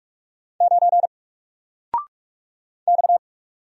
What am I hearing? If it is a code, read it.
Morse code, heard as a keyed tone, about 45 wpm, 700 Hz. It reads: 9 X
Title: Single Letter-Number Alt 45wpm